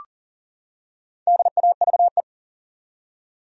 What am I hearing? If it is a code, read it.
Morse code, heard as a keyed tone, about 40 wpm, 700 Hz. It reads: DAVE
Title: USA Common Names - 40wpm